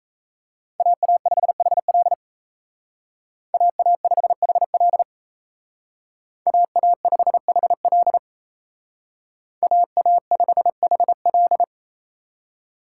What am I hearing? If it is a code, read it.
Morse code, heard as a keyed tone, about 34 wpm, 700 Hz. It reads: AA5HL AA5HL AA5HL AA5HL
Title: Speed Racing - USA 2x2 Call Signs 28wpm